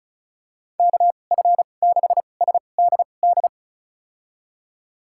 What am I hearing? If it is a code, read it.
Morse code, heard as a keyed tone, about 36 wpm, spaced as for 25 wpm, 700 Hz. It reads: KF6SDD
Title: USA Call Signs - 2x Character Spacing - 35wpm